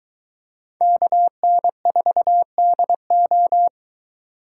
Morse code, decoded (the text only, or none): KN4DO